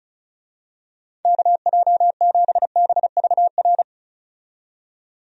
Morse code, heard as a keyed tone, about 36 wpm, 700 Hz. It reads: KJ7BVR
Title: USA Call Signs - 6 Characters - 35wpm